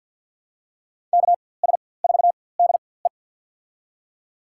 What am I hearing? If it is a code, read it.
Morse code, heard as a keyed tone, about 47 wpm, spaced as for 23 wpm, 700 Hz. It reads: KS4DE
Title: USA Call Signs - 4x Character Spacing - 50wpm